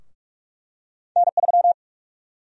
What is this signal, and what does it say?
Morse code, heard as a keyed tone, about 45 wpm, 700 Hz. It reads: N3
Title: USA Call Sign Prefixes - Repeated 2x - 45wpm